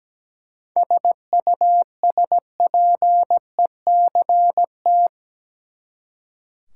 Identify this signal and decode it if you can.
Morse code, heard as a keyed tone, about 17 wpm, 700 Hz. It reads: SUSPECT
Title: Seven Letter Words - 17wpm